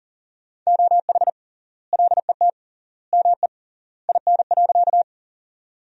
Morse code, heard as a keyed tone, about 40 wpm, 700 Hz. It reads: OH LET ME IN.
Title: 3x Word-Spacing - Sentences from Top 500 Words - 40wpm